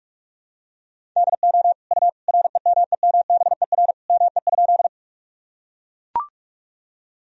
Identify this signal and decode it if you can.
Morse code, heard as a keyed tone, about 42 wpm, 700 Hz. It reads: DO U REMEMBER ME?
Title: Sentences from Top 400 Words and Abbreviations 45wpm